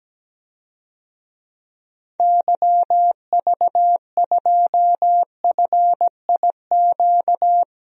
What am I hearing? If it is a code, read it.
Morse code, heard as a keyed tone, about 17 wpm, 700 Hz. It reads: YV2FIQ
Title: DXCC-100 Call Signs - 6 Characters - 17wpm